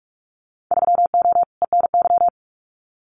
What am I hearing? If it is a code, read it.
Morse code, heard as a keyed tone, about 44 wpm, 690 Hz. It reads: 3O ENY
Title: Winter Field Day Exchanges Encore 45wpm